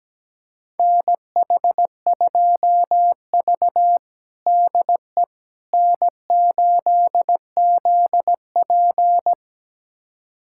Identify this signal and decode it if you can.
Morse code, heard as a keyed tone, about 17 wpm, 700 Hz. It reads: NH2V DE N8ZP
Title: USA Callsign DE Callsign - 4 Characters - 17wpm